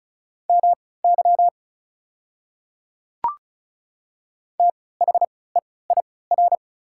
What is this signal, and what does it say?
Morse code, heard as a keyed tone, about 36 wpm, spaced as for 20 wpm, 690 Hz. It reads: MY THEIR
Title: 3x Character Spacing - Top 100 Words - 35wpm